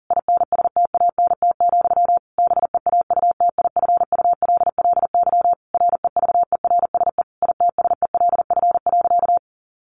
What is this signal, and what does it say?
Morse code, heard as a keyed tone, about 40 wpm, 700 Hz. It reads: INSTANT, BEAUTIFULLY REVERSE ITSELF.
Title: A Sound Of Thunder 40wpm